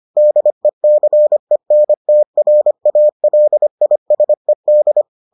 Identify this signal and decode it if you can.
Morse code, heard as a keyed tone, about 25 wpm, 600 Hz. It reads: DECENTRALISED